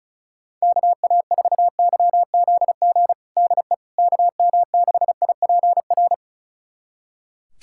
Morse code, heard as a keyed tone, about 36 wpm, 700 Hz. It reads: KA4YZG DE KM6IPR
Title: USA Callsign DE Callsign - 6 Characters - 35wpm